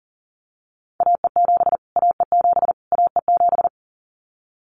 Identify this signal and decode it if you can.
Morse code, heard as a keyed tone, about 40 wpm, 700 Hz. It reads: AE7 AE7 AE7
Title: Callsign Prefixes for Top 25 DXCC - Repeated 3x - 40wpm